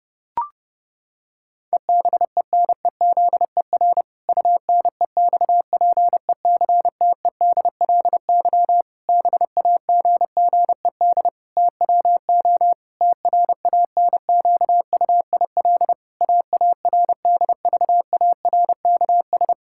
Morse code, heard as a keyed tone, about 32 wpm, 700 Hz. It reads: EBENEZER UNEXPECTEDLY BAGGED TWO TRANQUIL AARDVARKS
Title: Pangram Challenge v2 - 30wpm